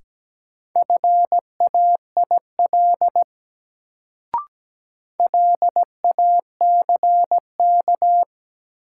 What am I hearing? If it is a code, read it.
Morse code, heard as a keyed tone, about 17 wpm, 700 Hz. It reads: FAIL LACK
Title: Sentences from Top 1100 Words - New Word Drill Alt - 17wpm